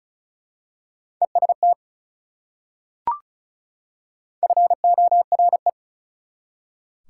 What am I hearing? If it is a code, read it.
Morse code, heard as a keyed tone, about 36 wpm, 690 Hz. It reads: EST FORE
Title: Prefixes and Suffixes 35wpm